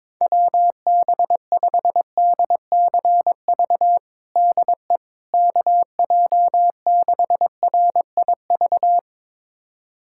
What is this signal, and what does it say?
Morse code, heard as a keyed tone, about 22 wpm, 700 Hz. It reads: WB5DCV DE KJ6RIV
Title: USA Callsign DE Callsign - 6 Characters - 22wpm